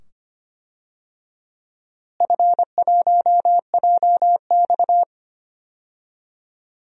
Morse code, heard as a keyed tone, about 25 wpm, 700 Hz. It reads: F1JX
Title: DXCC-100 Call Signs - 4 Characters - Repeated 2x - 25wpm